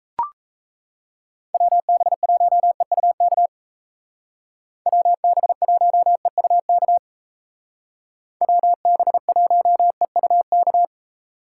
Morse code, heard as a keyed tone, about 37 wpm, 700 Hz. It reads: WB1EUK WB1EUK WB1EUK